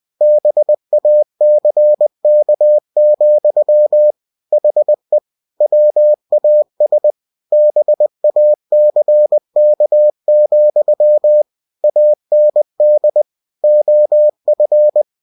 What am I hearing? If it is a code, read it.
Morse code, heard as a keyed tone, about 20 wpm, 600 Hz. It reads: BACK, HE WAS BACK, AND OF
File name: cw-books / winnie-the-pooh / winnie-20wpm-page-0018